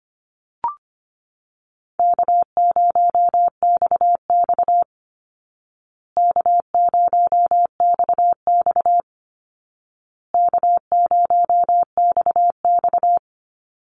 Morse code, heard as a keyed tone, about 25 wpm, 700 Hz. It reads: K0XX K0XX K0XX
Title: Easy Call Signs Repeated 3x - 25wpm